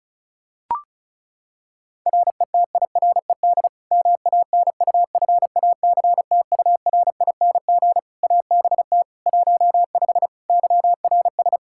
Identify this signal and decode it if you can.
Morse code, heard as a keyed tone, about 36 wpm, 700 Hz. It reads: RETIRED MANUFACTURING ABT 15 YRS
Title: Practice QSO - Exchange 3 - More Info - 35wpm